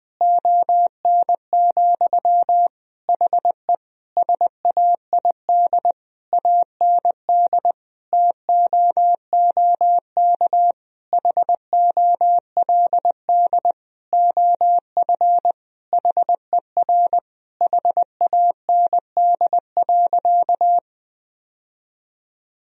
Morse code, heard as a keyed tone, about 20 wpm, 700 Hz. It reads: ON, HE SAID AND TOOK HOLD OF HER HAND.
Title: Sentences from Top 1000 Words - Review Part 2 - 20wpm